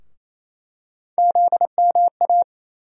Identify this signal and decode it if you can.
Morse code, heard as a keyed tone, about 28 wpm, 700 Hz. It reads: ZMA
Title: USA Call Sign Postfixes - 3 Letters 28wpm